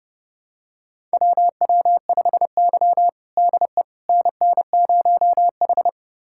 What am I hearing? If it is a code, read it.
Morse code, heard as a keyed tone, about 29 wpm, 700 Hz. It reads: WW5Y DE NN0H
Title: USA Callsign DE Callsign - 4 Characters - 30wpm